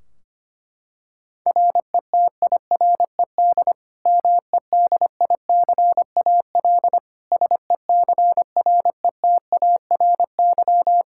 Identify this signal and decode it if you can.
Morse code, heard as a keyed tone, about 25 wpm, 700 Hz. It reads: RETIRED MEDICAL SECRETARY
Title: Practice QSO - Exchange 3 - More Info - 25wpm